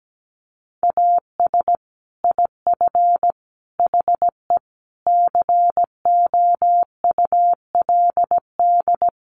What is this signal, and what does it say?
Morse code, heard as a keyed tone, about 17 wpm, 700 Hz. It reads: AS IF HE COULD
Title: Sentences from Top 1300 Words 17wpm